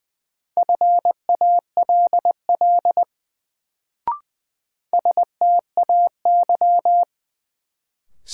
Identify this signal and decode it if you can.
Morse code, heard as a keyed tone, about 20 wpm, 700 Hz. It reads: FALL STAY
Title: Sentences from Top 500 Words - New Word Drill - 20wpm